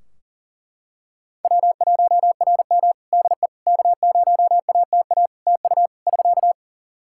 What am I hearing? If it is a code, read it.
Morse code, heard as a keyed tone, about 40 wpm, 700 Hz. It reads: W1RM DE K0ATA TU <SK>
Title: Practice QSO - Exchange 4 - Ending - 40wpm